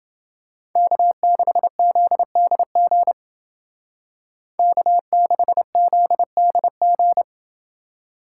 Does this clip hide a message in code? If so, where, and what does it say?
Yes: Morse code from 0.8 to 7.2 s, heard as a keyed tone, about 28 wpm, 700 Hz. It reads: K6ZDG K6ZDG